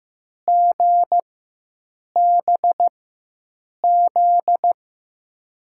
Morse code, heard as a keyed tone, about 16 wpm, spaced as for 7 wpm, 700 Hz. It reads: GBZ
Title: USA Call Signs - 6 Characters - 4x Character Spacing - 15wpm